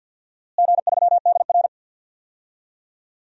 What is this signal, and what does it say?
Morse code, heard as a keyed tone, about 48 wpm, 700 Hz. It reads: G3DR
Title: DXCC-100 Call Signs - 4 Characters - Repeated 3x - 50wpm